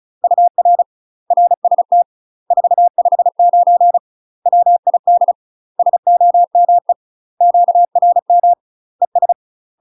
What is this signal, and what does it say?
Morse code, heard as a keyed tone, about 34 wpm, 700 Hz. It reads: UR RST 459 WID SOME QRM ES
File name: UK MASTER QSOs TRAINING / 35WPM / UK MASTER QSOs 0027